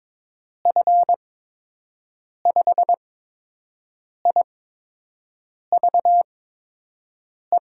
Morse code, heard as a keyed tone, about 21 wpm, spaced as for 6 wpm, 700 Hz. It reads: F5IVE
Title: USA Call Signs - 6 Characters - 8x Character Spacing - 22wpm